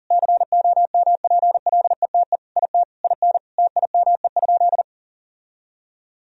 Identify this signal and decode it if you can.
Morse code, heard as a keyed tone, about 40 wpm, 700 Hz. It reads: COMPLETE IT IN TIME?
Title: Sentences from Top 1000 Words - Review Part 3 - 40wpm